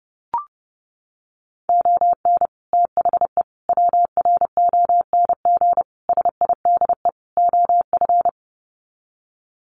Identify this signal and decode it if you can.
Morse code, heard as a keyed tone, about 30 wpm, 700 Hz. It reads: ON THE WRONG SIDE OF
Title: Sets of 5 Words Encore 30wpm